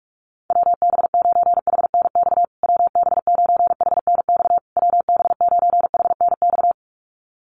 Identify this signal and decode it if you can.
Morse code, heard as a keyed tone, about 44 wpm, 700 Hz. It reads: WB9HNX WB9HNX WB9HNX
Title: USA Call Signs Repeated 3x 45wpm